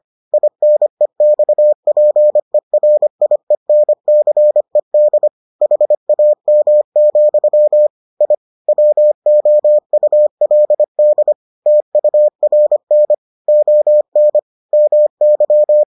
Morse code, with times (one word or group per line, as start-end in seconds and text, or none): 0.3-5.3 INEXPERIENCED
5.6-7.9 HAM,
8.2-8.3 I
8.7-11.3 WOULD
11.7-13.1 TURN
13.5-14.4 ON
14.7-15.8 MY